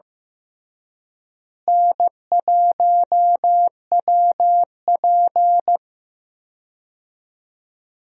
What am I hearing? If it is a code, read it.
Morse code, heard as a keyed tone, about 15 wpm, 700 Hz. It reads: N1WP
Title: USA Call Signs - 4 Characters - Repeated 3x - 15wpm